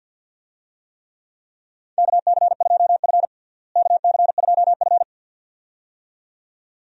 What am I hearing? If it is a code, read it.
Morse code, heard as a keyed tone, about 52 wpm, 700 Hz. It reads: KC2F KC2F
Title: USA Call Signs - 4 Characters - Repeated 2x - 50wpm